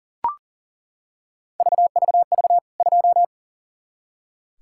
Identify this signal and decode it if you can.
Morse code, heard as a keyed tone, about 40 wpm, 700 Hz. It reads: VVV 2
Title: K1USN SST QSO Practice 40wpm